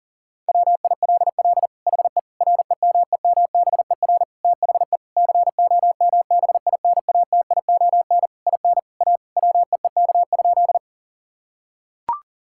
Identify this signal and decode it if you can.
Morse code, heard as a keyed tone, about 40 wpm, 700 Hz. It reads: WILL HE REMEMBER THE COMBINATION IN A WEEK?
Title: Sentences from Top 1000 Words - Review Part 6 - 40wpm